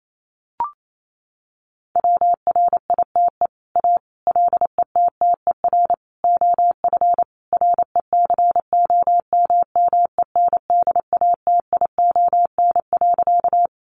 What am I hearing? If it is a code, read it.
Morse code, heard as a keyed tone, about 28 wpm, 700 Hz. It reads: WRITE A LETTER OF RECOMMENDATION.
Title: Sentences from Top 1400 Words 28wpm